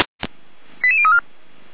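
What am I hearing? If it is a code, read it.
Five-tone selective call (ZVEI), heard as quick runs of five tones, tones of about 70 ms. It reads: 88034
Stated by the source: ZVEI five-tone call 88335, received by SDR, on the feuerwehr-orlamuende fire brigade server.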